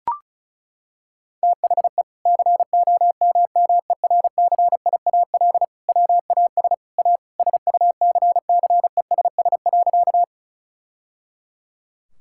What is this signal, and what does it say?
Morse code, heard as a keyed tone, about 34 wpm, 700 Hz. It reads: THE COMMERCIAL WAS A SUCCESS.
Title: Sentences from Top 1000 Words - Review Part 5 - 35wpm